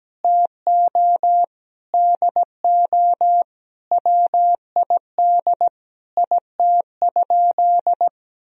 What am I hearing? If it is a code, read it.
Morse code, heard as a keyed tone, about 17 wpm, 700 Hz. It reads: TO DO WID IT?
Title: Sentences from Top 100 Words and Abbreviations 17wpm